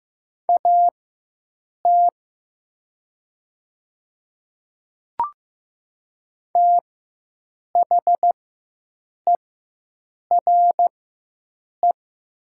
Morse code, heard as a keyed tone, about 15 wpm, spaced as for 7 wpm, 700 Hz. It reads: AT THERE
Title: 4x Character Spacing - Top 100 Words - 15wpm